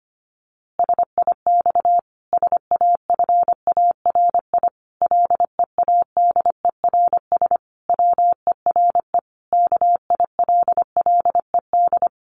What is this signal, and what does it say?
Morse code, heard as a keyed tone, about 25 wpm, 700 Hz. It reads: SIX SAFARI LEADERS WERE KILLED
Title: A Sound Of Thunder 25wpm